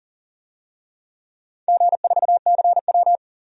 Morse code, heard as a keyed tone, about 40 wpm, 700 Hz. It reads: G4CW
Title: DXCC-100 Call Signs - 4 Characters - 40wpm